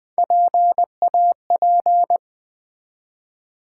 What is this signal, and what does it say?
Morse code, heard as a keyed tone, about 20 wpm, 700 Hz. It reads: PAP
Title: Build A Word - 1 Letter At A Time - 3 to 5-Letter Words - 20wpm